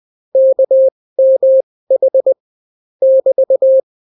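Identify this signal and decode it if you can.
Morse code, heard as a keyed tone, about 20 wpm, spaced as for 16 wpm, 530 Hz. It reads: KMH <BT>
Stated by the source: Daily Morse weather and top stories at 16 wpm, clean without QRN.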